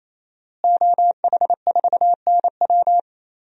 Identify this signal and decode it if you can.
Morse code, heard as a keyed tone, about 28 wpm, 700 Hz. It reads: OH4NW
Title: DXCC-100 Call Signs - 5 Characters - Repeated 3x - 28wpm